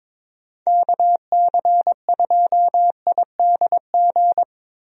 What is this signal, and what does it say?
Morse code, heard as a keyed tone, about 22 wpm, 700 Hz. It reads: KC2IDG
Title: USA Call Signs - 6 Characters - Repeated 2x - 22wpm